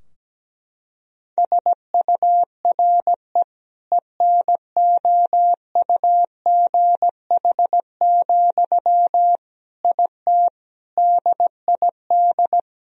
Morse code, heard as a keyed tone, about 17 wpm, 700 Hz. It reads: SURE ENOUGH, IT DID
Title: Sentences from Top 1000 Words - Review Part 2 - 17wpm